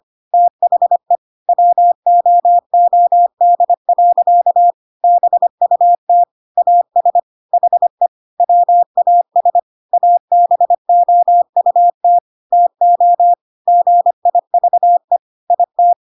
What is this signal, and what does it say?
Morse code, heard as a keyed tone, about 25 wpm, 700 Hz. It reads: THE WOOD. BUT AS HE WAS ABOUT TO GIVE IT
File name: Pinocchio / Pinocchio ch1 25 25 ik1hge